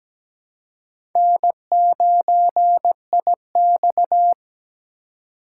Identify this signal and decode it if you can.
Morse code, heard as a keyed tone, about 17 wpm, 700 Hz. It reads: N9IX
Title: USA Call Signs - 4 Characters - 3x Character Spacing - 17wpm